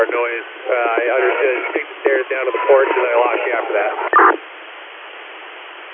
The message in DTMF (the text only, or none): none